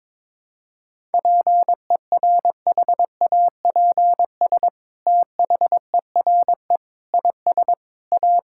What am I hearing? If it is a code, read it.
Morse code, heard as a keyed tone, about 22 wpm, 700 Hz. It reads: PERHAPS THERE IS A
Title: Sentences from Top 1000 Words - Review Part 5 - 22wpm